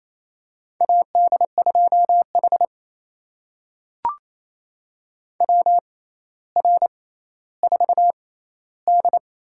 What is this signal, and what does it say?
Morse code, heard as a keyed tone, about 28 wpm, spaced as for 14 wpm, 700 Hz. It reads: AD2H WR4D
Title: USA Call Signs - 4 Characters - 6x Character Spacing - 28wpm